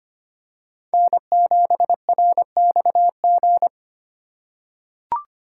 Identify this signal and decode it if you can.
Morse code, heard as a keyed tone, about 25 wpm, 700 Hz. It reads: N7RXG
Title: USA Call Signs - 5 Characters - 2x Character Spacing - 25wpm